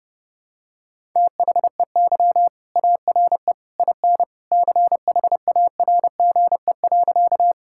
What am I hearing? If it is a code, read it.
Morse code, heard as a keyed tone, about 30 wpm, 700 Hz. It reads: THEY ARE IN CHARGE.
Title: Sentences from Top 1000 Words - Review Part 6 - 30wpm